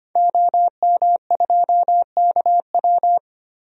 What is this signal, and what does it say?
Morse code, heard as a keyed tone, about 25 wpm, 700 Hz. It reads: OM2KW
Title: DXCC-100 Call Signs - 5 Characters - 25wpm